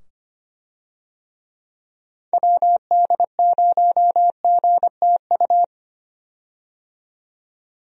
Morse code, heard as a keyed tone, about 25 wpm, 700 Hz. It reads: WD0GTU
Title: USA Call Signs - 6 Characters - 25wpm